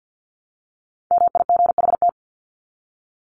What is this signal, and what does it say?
Morse code, heard as a keyed tone, about 50 wpm, 700 Hz. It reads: MIGHT
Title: Top 500 Words Repeated 50wpm